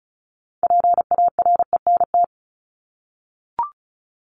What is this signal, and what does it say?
Morse code, heard as a keyed tone, about 34 wpm, 700 Hz. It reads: PARENT
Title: Top 1751-2000 Words 35wpm